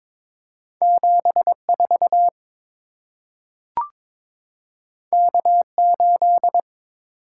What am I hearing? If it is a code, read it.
Morse code, heard as a keyed tone, about 22 wpm, 700 Hz. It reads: 74 K8
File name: practice / 22wpm / Two Letter-Number Combinations - 22wpm